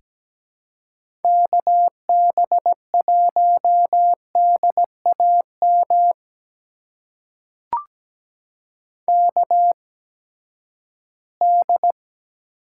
Morse code, heard as a keyed tone, about 17 wpm, spaced as for 12 wpm, 700 Hz. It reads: KB1DAM KD
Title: USA Call Signs - 6 Characters - 8x Character Spacing - 17wpm